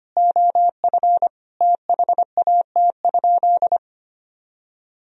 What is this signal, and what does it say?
Morse code, heard as a keyed tone, about 25 wpm, 700 Hz. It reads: OF THAT?
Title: Sentences from Top 100 Words and Abbreviations 25wpm